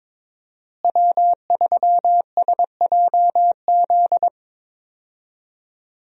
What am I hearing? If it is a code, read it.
Morse code, heard as a keyed tone, about 22 wpm, 700 Hz. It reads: W3SJZ
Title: USA Call Signs - 5 Characters - Repeated 2x - 22wpm